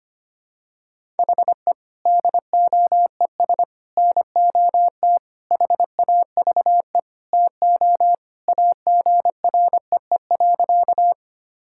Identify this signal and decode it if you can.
Morse code, heard as a keyed tone, about 25 wpm, 700 Hz. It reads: HE DOES NOT HAVE TO AGREE.
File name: practice / 25wpm / Sentences with 4-Word Phrases - 25wpm